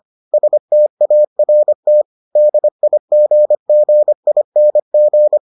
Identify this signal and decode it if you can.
Morse code, heard as a keyed tone, about 25 wpm, 600 Hz. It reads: START DIGGING